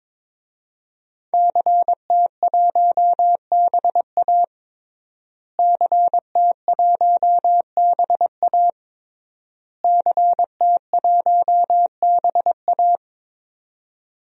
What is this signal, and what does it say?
Morse code, heard as a keyed tone, about 22 wpm, 700 Hz. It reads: CT1BA CT1BA CT1BA
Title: DXCC-100 Call Signs - 5 Characters - Repeated 3x - 22wpm